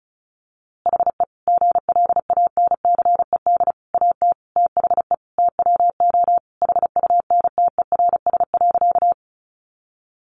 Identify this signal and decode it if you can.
Morse code, heard as a keyed tone, about 34 wpm, 700 Hz. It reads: HE GLANCED AT THE TWO HUNTERS.
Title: A Sound Of Thunder 35wpm